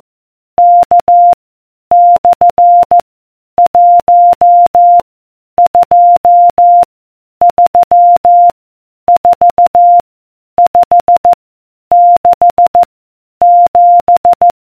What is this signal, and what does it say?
Morse code, heard as a keyed tone, about 14 wpm, 700 Hz. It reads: K / 1 2 3 4 5 6 7